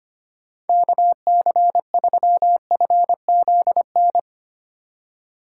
Morse code, heard as a keyed tone, about 24 wpm, 700 Hz. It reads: KC3FZN
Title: Speed Racing - USA 2x3 Call Signs 25wpm